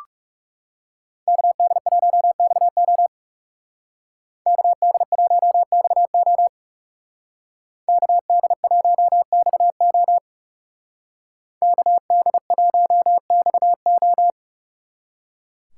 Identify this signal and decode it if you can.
Morse code, heard as a keyed tone, about 39 wpm, 700 Hz. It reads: KD1XO KD1XO KD1XO KD1XO